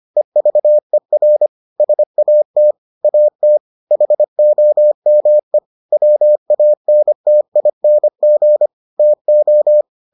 Morse code, heard as a keyed tone, about 25 wpm, 600 Hz. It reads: EVER SAT AT HOME WANTING TO